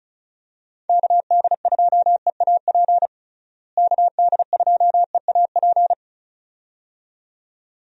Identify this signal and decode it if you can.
Morse code, heard as a keyed tone, about 36 wpm, 700 Hz. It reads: KD2EAP KD2EAP
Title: USA Call Signs - 6 Characters - Repeated 2x - 35wpm